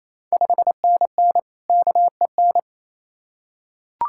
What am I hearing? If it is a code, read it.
Morse code, heard as a keyed tone, about 28 wpm, 700 Hz. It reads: 5NN KEN
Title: Ontario QSO Party Exchange 28wpm